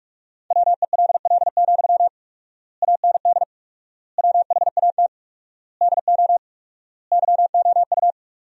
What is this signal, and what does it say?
Morse code, heard as a keyed tone, about 47 wpm, 700 Hz. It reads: WELL, AND WHAT DO YOU
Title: 4x Word-Spacing - Sentences from Top 200 Words - 45wpm